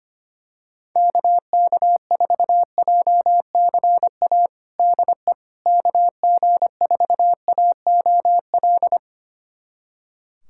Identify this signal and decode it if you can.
Morse code, heard as a keyed tone, about 25 wpm, 700 Hz. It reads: KK4JCA DE KG4AOL
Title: USA Callsign DE Callsign - 6 Characters - 25wpm